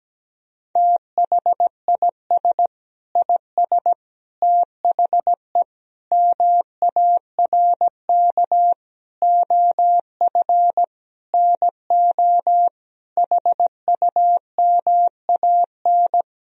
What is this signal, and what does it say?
Morse code, heard as a keyed tone, about 17 wpm, 700 Hz. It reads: THIS IS THE MARK OF NO HUMAN
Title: Sentences from Top 1000 Words - Review Part 3 - 17wpm